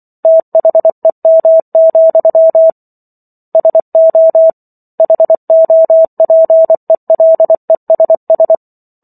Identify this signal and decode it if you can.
Morse code, heard as a keyed tone, about 24 wpm, 650 Hz. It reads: THEM, SO HOPELESS